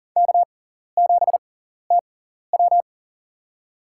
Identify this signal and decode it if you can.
Morse code, heard as a keyed tone, about 40 wpm, spaced as for 14 wpm, 700 Hz. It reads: K7TW
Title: USA Call Signs - 4 Characters - 6x Character Spacing - 40wpm